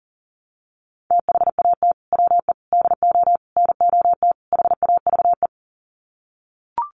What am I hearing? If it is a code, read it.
Morse code, heard as a keyed tone, about 40 wpm, 700 Hz. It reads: THAT WE DO NOT HAVE
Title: Sets of 5 Words Encore 40wpm